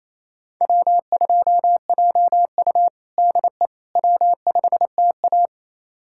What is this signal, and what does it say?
Morse code, heard as a keyed tone, about 27 wpm, 700 Hz. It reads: W2JU DE W5TA
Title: USA Callsign DE Callsign - 4 Characters - 28wpm